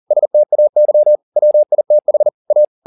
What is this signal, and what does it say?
Morse code, heard as a keyed tone, about 40 wpm, 600 Hz. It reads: STAY WITH A